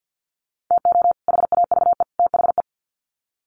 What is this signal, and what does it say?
Morse code, heard as a keyed tone, about 50 wpm, 700 Hz. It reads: TO HAVE THE
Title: Sets of 3 Words 50wpm